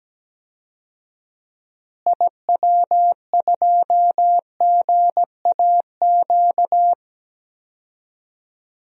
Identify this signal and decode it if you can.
Morse code, heard as a keyed tone, about 17 wpm, 700 Hz. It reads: IW2GAQ
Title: DXCC-100 Call Signs - 6 Characters - Repeated 3x - 17wpm